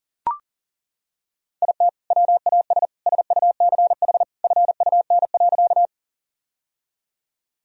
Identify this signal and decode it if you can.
Morse code, heard as a keyed tone, about 40 wpm, 700 Hz. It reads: IT WAS SUCH FUN.
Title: Sentences from Top 1000 Words - Review Part 4 - 40wpm